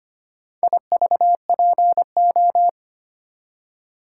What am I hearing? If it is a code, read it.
Morse code, heard as a keyed tone, about 25 wpm, 700 Hz. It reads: IVPO
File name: practice / 25wpm / Four Letter-Number Combinations - 25wpm